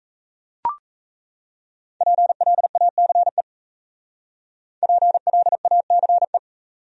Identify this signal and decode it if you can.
Morse code, heard as a keyed tone, about 40 wpm, 700 Hz. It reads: PLACE PLACE